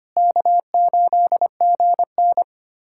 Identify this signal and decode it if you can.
Morse code, heard as a keyed tone, about 25 wpm, 700 Hz. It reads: K8GN
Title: USA Callsign DE Callsign 25wpm